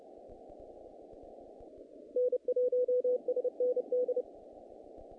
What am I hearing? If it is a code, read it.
Morse code, heard as a keyed tone, about 30 wpm, 500 Hz. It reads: N1SND
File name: contests / naqp-cw-k5tr-op-k5pi- -2100